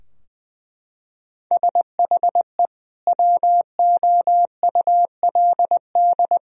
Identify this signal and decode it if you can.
Morse code, heard as a keyed tone, about 21 wpm, 700 Hz. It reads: SHE WOULD
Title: Sentences from Top 1000 Words - Review Part 4 - 20wpm